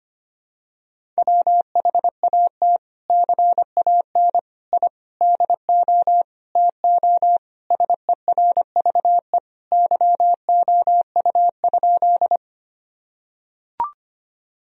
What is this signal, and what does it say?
Morse code, heard as a keyed tone, about 25 wpm, 700 Hz. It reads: WHAT CAN I DO TO SERVE YOU?
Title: Sentences from Top 1000 Words - Review Part 2 - 25wpm